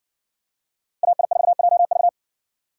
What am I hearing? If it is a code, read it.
Morse code, heard as a keyed tone, about 60 wpm, 700 Hz. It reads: AI4JV